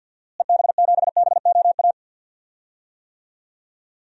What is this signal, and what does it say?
Morse code, heard as a keyed tone, about 51 wpm, 700 Hz. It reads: EB7BOA
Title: DXCC-100 Call Signs - 6 Characters - Repeated 3x - 50wpm